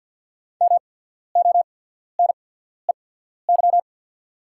Morse code, heard as a keyed tone, about 48 wpm, spaced as for 14 wpm, 690 Hz. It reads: MONEY
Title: 8x Character Spacing - Top 500 Words - 50wpm